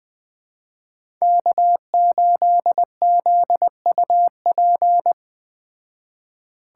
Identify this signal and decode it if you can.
Morse code, heard as a keyed tone, about 21 wpm, 700 Hz. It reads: K8ZUP